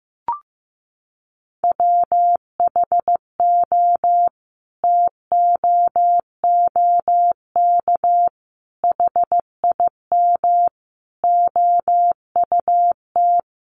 Morse code, heard as a keyed tone, about 15 wpm, 700 Hz. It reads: WHO TOOK HIM OUT
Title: Sentences from Top 300 Words 15wpm